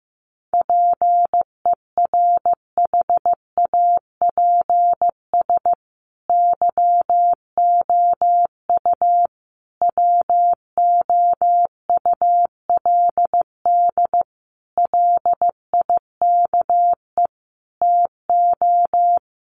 Morse code, heard as a keyed tone, about 15 wpm, 700 Hz. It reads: PERHAPS YOU WOULD LIKE TO
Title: Sentences from Top 500 Words 15wpm